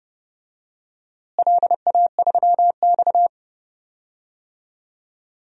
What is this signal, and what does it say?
Morse code, heard as a keyed tone, about 30 wpm, 700 Hz. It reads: LA3X
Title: DXCC-100 Call Signs - 4 Characters - Repeated 2x - 30wpm